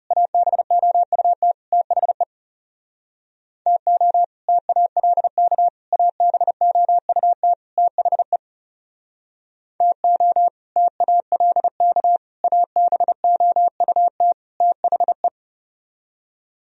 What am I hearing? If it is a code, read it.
Morse code, heard as a keyed tone, about 37 wpm, 700 Hz. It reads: ABOUT THE TO TALK ABOUT THE TO TALK ABOUT THE
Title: Speed-Racing - Sets of 4 Words 30wpm